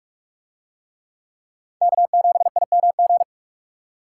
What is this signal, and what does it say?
Morse code, heard as a keyed tone, about 42 wpm, 700 Hz. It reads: K7IMG